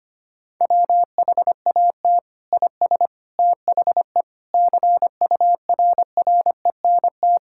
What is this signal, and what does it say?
Morse code, heard as a keyed tone, about 25 wpm, 700 Hz. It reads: WHAT IS THE CURRENT
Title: Sentences from Top 1000 Words - Review Part 6 - 25wpm